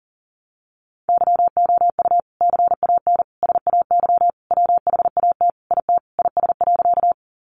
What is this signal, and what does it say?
Morse code, heard as a keyed tone, about 40 wpm, 700 Hz. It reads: YOU CAN SAY WHAT IT IS.
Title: Sentences from Top 200 Words 40wpm